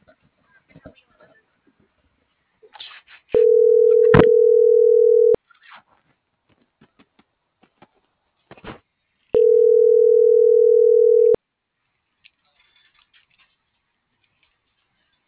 Ringback tone (North America), 440 + 480 Hz.
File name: CENTRAL TERRESTRE / out cris.f